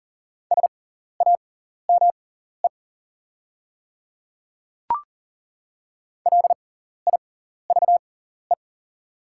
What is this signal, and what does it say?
Morse code, heard as a keyed tone, about 41 wpm, spaced as for 14 wpm, 700 Hz. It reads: SAME LIVE
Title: 6x Character Spacing - Top 500 Words - 40wpm